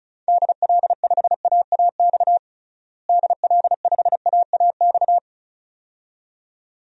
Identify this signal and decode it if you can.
Morse code, heard as a keyed tone, about 36 wpm, 700 Hz. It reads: DL5AAX DL5AAX
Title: DXCC-100 Call Signs - 6 Characters - Repeated 2x - 35wpm